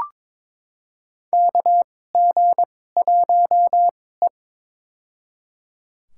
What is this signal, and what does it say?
Morse code, heard as a keyed tone, about 22 wpm, spaced as for 16 wpm, 700 Hz. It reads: KG1E
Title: USA Call Signs - 4 Characters - 2x Character Spacing - 22wpm